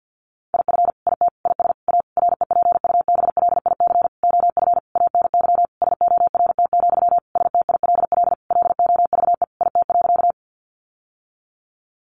Morse code, heard as a keyed tone, about 50 wpm, 700 Hz. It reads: IF IT IS A REPUBLIC OF ANY SORT, STILL LOVE IT.
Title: Sentences from Top 1500 Words 50wpm